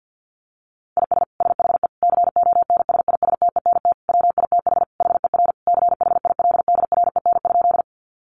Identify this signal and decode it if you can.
Morse code, heard as a keyed tone, about 48 wpm, 700 Hz. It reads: IS SHE CONSISTENT WITH HER CHILDREN?
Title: Sentences from Top 1500 Words 50wpm